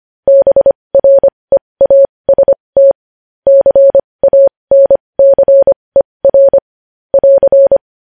Morse code, heard as a keyed tone, about 25 wpm, spaced as for 20 wpm, 550 Hz. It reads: BREAST CANCER <AR>
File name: podcast / News headlines-1x-25x20